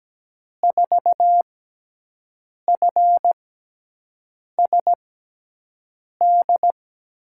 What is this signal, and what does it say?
Morse code, heard as a keyed tone, about 17 wpm, spaced as for 6 wpm, 700 Hz. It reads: 4FSD
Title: USA Call Signs - 6 Characters - 6x Character Spacing - 17wpm